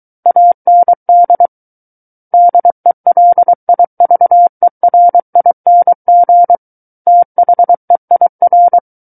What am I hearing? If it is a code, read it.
Morse code, heard as a keyed tone, about 23 wpm, 700 Hz. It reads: AND DELIVERING THEIR
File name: Book One / 14-Chapter 14 - 23 WPM